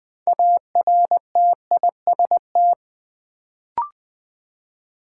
Morse code, heard as a keyed tone, about 20 wpm, 700 Hz. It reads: ARTIST
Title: Build A Word - 2 Letters At A Time - 6-Letter Words - 20wpm